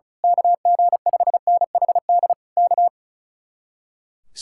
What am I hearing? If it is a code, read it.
Morse code, heard as a keyed tone, about 33 wpm, 700 Hz. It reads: KG5NHD K
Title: Speed-Racing Common QSO Phrases - 35wpm